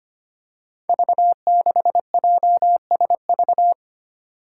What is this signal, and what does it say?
Morse code, heard as a keyed tone, about 25 wpm, 700 Hz. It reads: V6JSV
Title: Five Challenging Letter-Number Combinations 25wpm